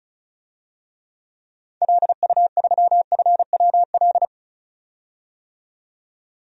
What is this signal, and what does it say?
Morse code, heard as a keyed tone, about 34 wpm, 700 Hz. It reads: LU3FWL
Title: DXCC-100 Call Signs - 6 Characters - 35wpm